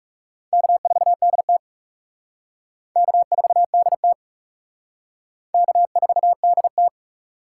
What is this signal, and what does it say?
Morse code, heard as a keyed tone, about 36 wpm, 700 Hz. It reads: K4DT K4DT K4DT
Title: Speed Racing - USA 1x2 Call Signs 30wpm